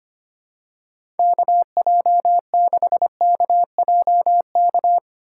Morse code, heard as a keyed tone, about 25 wpm, 700 Hz. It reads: KJ6KJK
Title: USA Call Signs - 6 Characters - 3x Character Spacing - 25wpm